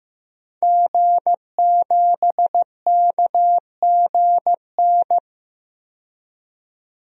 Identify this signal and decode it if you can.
Morse code, heard as a keyed tone, about 16 wpm, 700 Hz. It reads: G7KGN
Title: DXCC-100 Call Signs - 5 Characters - 15wpm